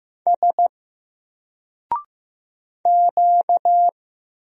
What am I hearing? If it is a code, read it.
Morse code, heard as a keyed tone, about 15 wpm, 700 Hz. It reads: S Q
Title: Single Letter-Number Alt 15wpm